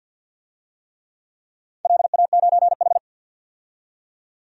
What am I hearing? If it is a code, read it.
Morse code, heard as a keyed tone, about 47 wpm, 700 Hz. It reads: LA9H
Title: DXCC-100 Call Signs - 4 Characters - 50wpm